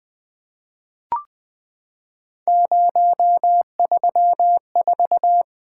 Morse code, heard as a keyed tone, about 20 wpm, 700 Hz. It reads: 034
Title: Three Letter-Number Combinations - 20wpm